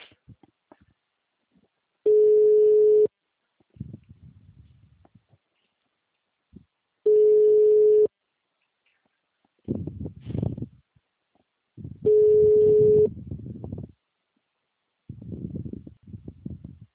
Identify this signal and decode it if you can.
Ringback tone (Europe, 425 Hz).
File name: out- -2017- - - .16